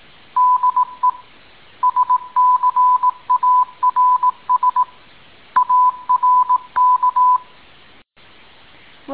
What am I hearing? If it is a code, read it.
Morse code, heard as a keyed tone, about 18 wpm, 1000 Hz. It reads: DE SCARS ARK